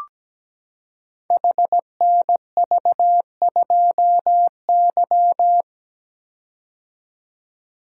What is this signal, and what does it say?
Morse code, heard as a keyed tone, about 17 wpm, 700 Hz. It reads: HNV2Y
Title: Code Groups - 5 Letter:Numbers - 17wpm